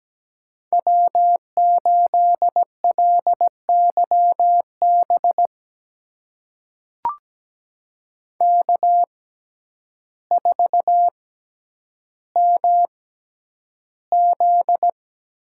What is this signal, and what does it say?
Morse code, heard as a keyed tone, about 17 wpm, spaced as for 9 wpm, 700 Hz. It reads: W8LYB K4MZ
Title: USA Call Signs - 5 Characters - 6x Character Spacing - 17wpm